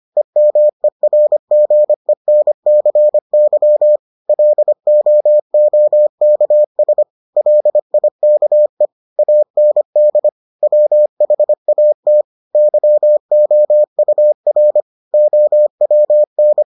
Morse code, heard as a keyed tone, about 25 wpm, 600 Hz. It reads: EMERGENCY LOOKS LIKE AND WHAT YOUR OWN